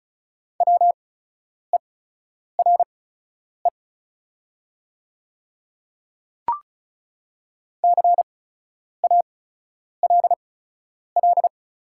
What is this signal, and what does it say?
Morse code, heard as a keyed tone, about 36 wpm, spaced as for 10 wpm, 690 Hz. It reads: WERE CALL
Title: 8x Character Spacing - Top 100 Words - 35wpm